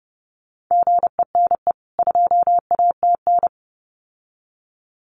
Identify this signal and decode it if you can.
Morse code, heard as a keyed tone, about 30 wpm, 700 Hz. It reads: GENE 2ATN
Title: CWT Exchange 30wpm